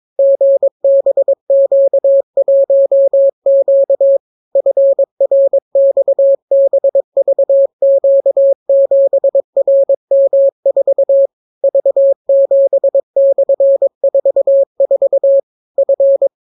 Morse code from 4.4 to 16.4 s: FRXBVQ7RM4 V7/44 F